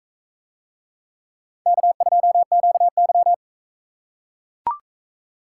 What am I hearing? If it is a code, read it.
Morse code, heard as a keyed tone, about 42 wpm, 700 Hz. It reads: K2QY